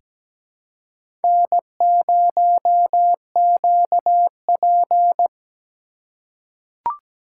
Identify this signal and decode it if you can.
Morse code, heard as a keyed tone, about 17 wpm, 700 Hz. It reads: N0QP
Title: USA Call Signs - 4 Characters - 6x Character Spacing - 17wpm